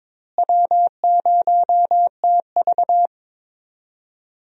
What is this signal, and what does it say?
Morse code, heard as a keyed tone, about 22 wpm, 700 Hz. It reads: W0TV
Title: USA Call Signs - 4 Characters - Repeated 3x - 22wpm